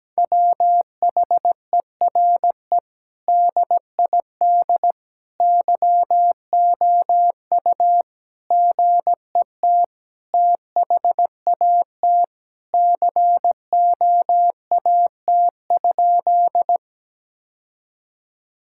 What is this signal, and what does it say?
Morse code, heard as a keyed tone, about 17 wpm, 700 Hz. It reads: WHERE DID YOU GET THAT COAT?
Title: Sentences from Top 1600 Words 17wpm